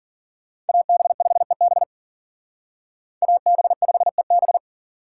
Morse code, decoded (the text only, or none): AB5EB AB5EB